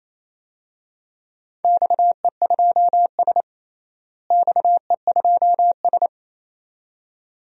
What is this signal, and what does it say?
Morse code, heard as a keyed tone, about 28 wpm, 700 Hz. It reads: XE2S XE2S
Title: DXCC-100 Call Signs - 4 Characters - Repeated 2x - 28wpm